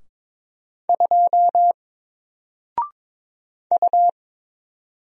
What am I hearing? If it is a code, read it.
Morse code, heard as a keyed tone, about 22 wpm, 700 Hz. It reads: 2 U
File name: Single-Letter-Number-22wpm